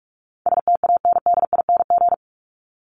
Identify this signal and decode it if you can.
Morse code, heard as a keyed tone, about 44 wpm, 700 Hz. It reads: STANDING
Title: ING Suffix - Top 100 Words with Base Words 45wpm